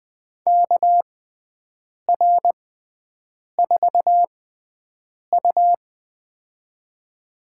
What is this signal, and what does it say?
Morse code, heard as a keyed tone, about 21 wpm, spaced as for 7 wpm, 700 Hz. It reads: KR4U
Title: USA Call Signs - 4 Characters - 6x Character Spacing - 20wpm